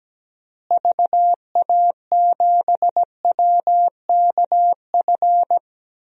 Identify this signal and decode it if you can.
Morse code, heard as a keyed tone, about 17 wpm, 700 Hz. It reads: VA7WKF
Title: DXCC-100 Call Signs - 6 Characters - Repeated 3x - 17wpm